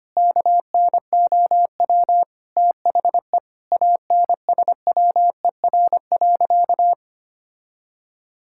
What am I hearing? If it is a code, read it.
Morse code, heard as a keyed tone, about 25 wpm, 700 Hz. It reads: KNOW THE ANSWER.
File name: practice / 25wpm / Sentences with 4-Word Phrases - 25wpm